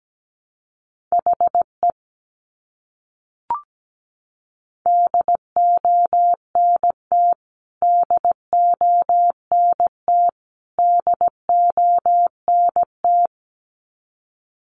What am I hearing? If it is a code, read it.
Morse code, heard as a keyed tone, about 17 wpm, 700 Hz. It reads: HE DONT DONT DONT
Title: Top 500 Words Repeated 17wpm